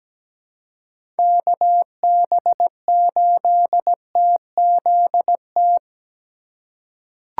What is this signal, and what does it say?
Morse code, heard as a keyed tone, about 17 wpm, 700 Hz. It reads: KB8TZT
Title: USA Call Signs - 3x Character Spacing - 17wpm